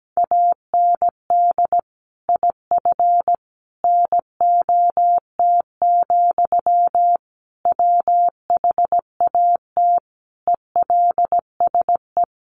Morse code, heard as a keyed tone, about 17 wpm, 700 Hz. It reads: AND IF NOT, WHAT ELSE
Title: Sentences from Top 1400 Words 17wpm